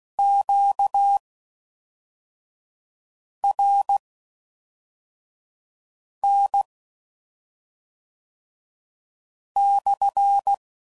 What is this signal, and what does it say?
Morse code, heard as a keyed tone, about 16 wpm, spaced as for 3 wpm, 800 Hz. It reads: QRN/